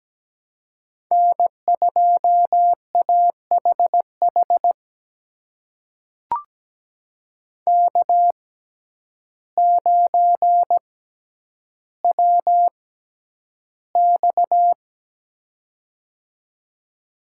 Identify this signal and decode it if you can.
Morse code, heard as a keyed tone, about 17 wpm, spaced as for 9 wpm, 700 Hz. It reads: N2AHH K9WX
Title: USA Call Signs - 6x Character Spacing - 17wpm